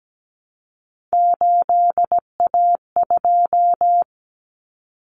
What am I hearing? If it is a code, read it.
Morse code, heard as a keyed tone, about 17 wpm, 700 Hz. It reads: 8A2
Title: Common Cut Numbers - 3-Digit 17wpm